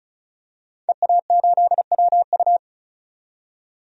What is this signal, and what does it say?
Morse code, heard as a keyed tone, about 36 wpm, 700 Hz. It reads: EA8WU
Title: DXCC-100 Call Signs - 5 Characters - 35wpm